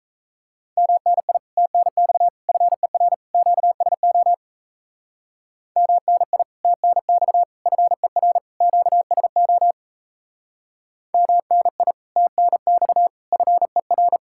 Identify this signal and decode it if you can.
Morse code, heard as a keyed tone, about 32 wpm, 700 Hz. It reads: MNI TNX FER QSO MNI TNX FER QSO MNI TNX FER
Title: Speed-Racing Common QSO Phrases Encore - 28wpm